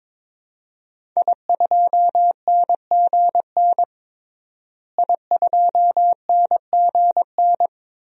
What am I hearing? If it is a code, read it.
Morse code, heard as a keyed tone, about 22 wpm, 700 Hz. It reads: I2NGN I2NGN
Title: DXCC-100 Call Signs - 5 Characters - Repeated 2x - 22wpm